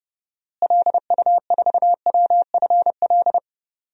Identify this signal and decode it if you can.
Morse code, heard as a keyed tone, about 32 wpm, 700 Hz. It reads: LU4WFL
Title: DXCC-100 Call Signs - 6 Characters - Repeated 2x - 30wpm